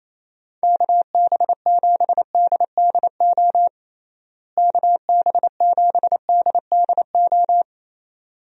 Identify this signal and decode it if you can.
Morse code, heard as a keyed tone, about 28 wpm, 700 Hz. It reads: KB7DDO KB7DDO
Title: USA Call Signs - 6 Characters - Repeated 2x - 28wpm